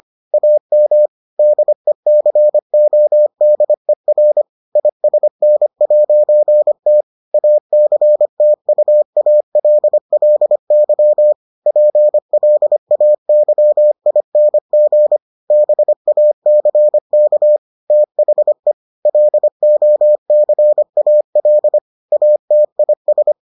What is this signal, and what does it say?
Morse code, heard as a keyed tone, about 25 wpm, 600 Hz. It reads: AM DECODER ISN'T ACTUALLY PLAYING BACK THE LOCAL ATIS